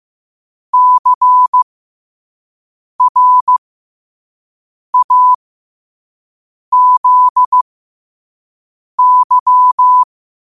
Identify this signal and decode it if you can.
Morse code, heard as a keyed tone, about 15 wpm, spaced as for 5 wpm, 1000 Hz. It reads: CRAZY
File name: qotdfiles / 5wpm